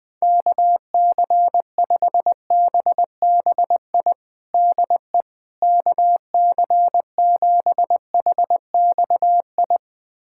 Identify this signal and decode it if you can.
Morse code, heard as a keyed tone, about 21 wpm, 700 Hz. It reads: KC5BBI DE KC7HXI
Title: USA Callsign DE Callsign - 6 Characters - 20wpm